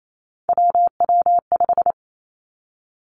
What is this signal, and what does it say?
Morse code, heard as a keyed tone, about 28 wpm, 700 Hz. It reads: WW5
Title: Three Challenging Letter-Number Combinations 28wpm